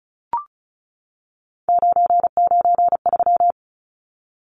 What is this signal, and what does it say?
Morse code, heard as a keyed tone, about 34 wpm, 700 Hz. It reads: 993
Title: Numbers - 3-Digit 35wpm